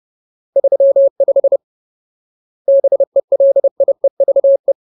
Morse code, heard as a keyed tone, about 30 wpm, 550 Hz. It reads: 35 BELIEVE
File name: Ultimate Word List / 30WPM Ultimate